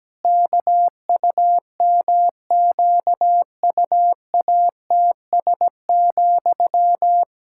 Morse code, heard as a keyed tone, about 17 wpm, 700 Hz. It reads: KUMQUATS,
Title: Pangram Challenge v2 - 17wpm